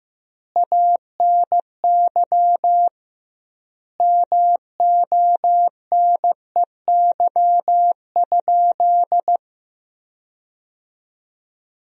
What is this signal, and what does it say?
Morse code, heard as a keyed tone, about 16 wpm, 700 Hz. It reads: ANY MONEY?
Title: 2x Word-Spacing - Sentences from Top 400 Words - 15wpm